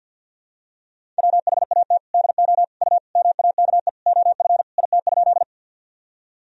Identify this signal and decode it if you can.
Morse code, heard as a keyed tone, about 52 wpm, 700 Hz. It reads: WHAT DO U MAKE OF IT?
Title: Sentences from Top 100 Words and Abbreviations 50wpm